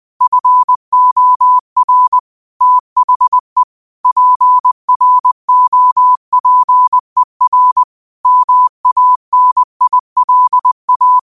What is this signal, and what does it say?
Morse code, heard as a keyed tone, about 20 wpm, 1000 Hz. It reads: FOR THE PROPER MANILA
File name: qotdfiles / 20wpm